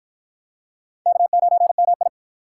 Morse code, heard as a keyed tone, about 53 wpm, 700 Hz. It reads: K9MI